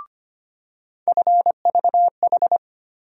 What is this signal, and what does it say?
Morse code, heard as a keyed tone, about 25 wpm, 700 Hz. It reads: FVH
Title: USA Call Sign Postfixes - 3 Letters 25wpm